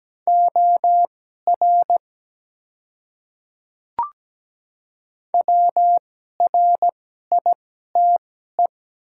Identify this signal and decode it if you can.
Morse code, heard as a keyed tone, about 17 wpm, spaced as for 12 wpm, 700 Hz. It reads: OR WRITE
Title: 2x Character Spacing - Top 100 Words - 17wpm